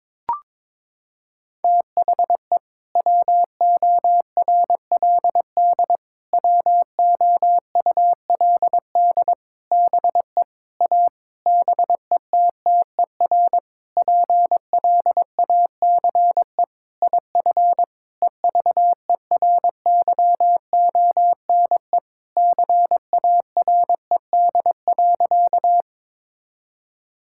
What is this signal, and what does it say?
Morse code, heard as a keyed tone, about 22 wpm, 700 Hz. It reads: THE WORLD WOULD BE A BETTER PLACE IF EVERYONE CARED.
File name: practice / 22wpm / Sentences with 8-Word Phrases - 22wpm